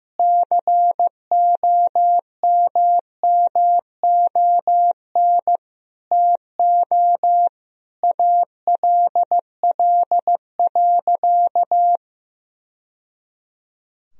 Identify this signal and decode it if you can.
Morse code, heard as a keyed tone, about 15 wpm, 700 Hz. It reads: COMMON TO ALL.
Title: Sentences from Top 1000 Words - Review Part 3 - 15wpm